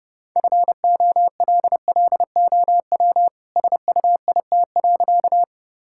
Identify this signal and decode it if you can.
Morse code, heard as a keyed tone, about 30 wpm, 700 Hz. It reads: FOLLOW SUIT.
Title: Sentences from Top 1600 Words 30wpm